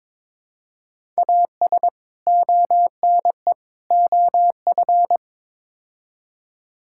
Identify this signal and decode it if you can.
Morse code, heard as a keyed tone, about 22 wpm, 700 Hz. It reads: AS ONE OF
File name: practice / 22wpm / Build A Phrase - 5-Word Phrases - 22wpm